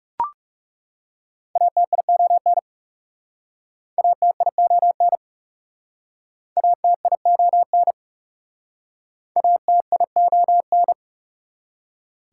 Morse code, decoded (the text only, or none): ATION ATION ATION ATION